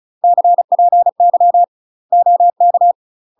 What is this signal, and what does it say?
Morse code, heard as a keyed tone, about 34 wpm, 700 Hz. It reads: CPY OK
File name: UK MASTER QSOs TRAINING / 35WPM / UK MASTER QSOs 0003